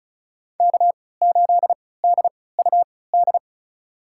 Morse code, heard as a keyed tone, about 36 wpm, spaced as for 20 wpm, 700 Hz. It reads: K8DUD
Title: USA Call Signs - 5 Characters - 3x Character Spacing - 35wpm